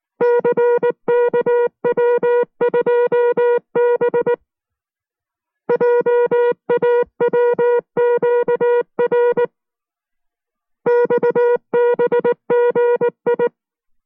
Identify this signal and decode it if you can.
Morse code, heard as a keyed tone, about 19 wpm, 480 Hz. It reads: CKW2B JAWQR XBGI